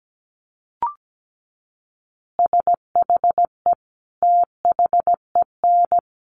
Morse code, heard as a keyed tone, about 17 wpm, 700 Hz. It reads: SHE THEN
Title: Easy Phrases 17wpm